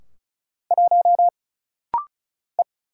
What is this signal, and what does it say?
Morse code, heard as a keyed tone, about 36 wpm, 700 Hz. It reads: 1 E
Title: Single Letter-Number Alt - Rapid-Fire 35wpm